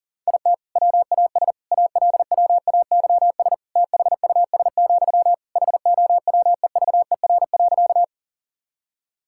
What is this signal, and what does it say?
Morse code, heard as a keyed tone, about 40 wpm, 700 Hz. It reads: IT WAS ALWAYS THUS, HOWEVER.
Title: Sentences from Top 1000 Words - Review Part 5 - 40wpm